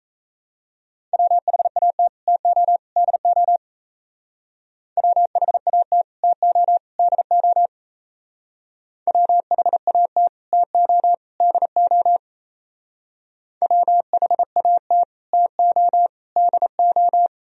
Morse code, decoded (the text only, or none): WHAT TO DO WHAT TO DO WHAT TO DO WHAT TO DO